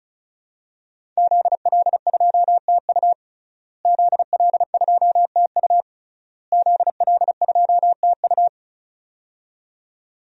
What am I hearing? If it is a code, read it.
Morse code, heard as a keyed tone, about 34 wpm, 700 Hz. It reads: ZL2TU ZL2TU ZL2TU
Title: DXCC-100 Call Signs - 5 Characters - Repeated 3x - 35wpm